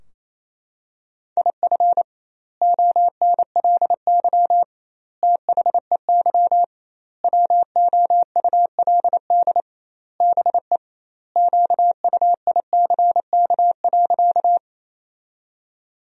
Morse code, heard as a keyed tone, about 28 wpm, 700 Hz. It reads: IF ONLY THEY WOULD BE QUICK.
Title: 2x Word-Spacing - Sentences from Top 500 Words - 28wpm